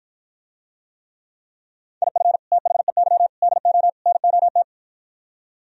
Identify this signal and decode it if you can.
Morse code, heard as a keyed tone, about 52 wpm, 700 Hz. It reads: IF THEY DO NOT